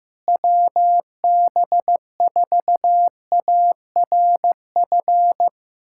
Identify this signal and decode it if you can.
Morse code, heard as a keyed tone, about 15 wpm, 700 Hz. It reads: WB4ARF
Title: USA Callsign DE Callsign 15wpm